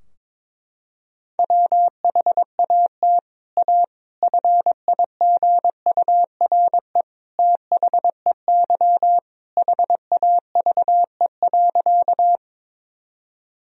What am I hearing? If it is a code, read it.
Morse code, heard as a keyed tone, about 22 wpm, 700 Hz. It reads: WHAT A FIGURE THEY HAVE.
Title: Sentences from Top 1000 Words - Review Part 2 - 22wpm